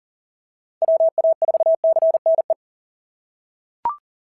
Morse code, heard as a keyed tone, about 36 wpm, 660 Hz. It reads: WA4CNE
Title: SOTA QSO Practice 40wpm